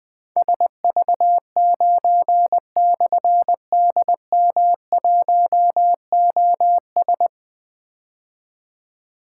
Morse code, heard as a keyed tone, about 20 wpm, 700 Hz. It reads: SV9/DM1OS
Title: Difficult Call Signs 20wpm